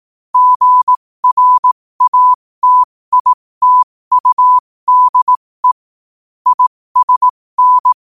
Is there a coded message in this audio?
Yes: Morse code from 0.3 to 7.9 s, heard as a keyed tone, about 18 wpm, spaced as for 15 wpm, 1000 Hz. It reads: GRATITUDE ISN